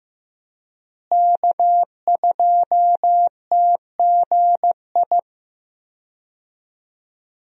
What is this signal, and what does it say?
Morse code, heard as a keyed tone, about 15 wpm, 700 Hz. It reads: K2TGI
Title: USA Call Signs - 5 Characters - Repeated 3x - 15wpm